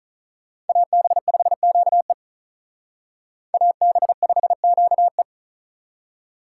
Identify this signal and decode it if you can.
Morse code, heard as a keyed tone, about 45 wpm, 700 Hz. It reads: AB5QE AB5QE